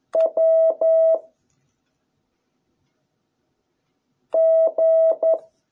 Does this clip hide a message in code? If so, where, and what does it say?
Yes: Morse code from 0.1 to 5.3 s, heard as a keyed tone, about 11 wpm, 660 Hz. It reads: W G